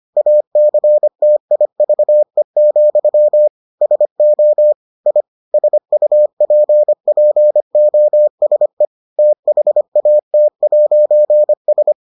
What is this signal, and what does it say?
Morse code, heard as a keyed tone, about 25 wpm, 600 Hz. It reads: ACTIVE, SO I SUPPOSE THAT'S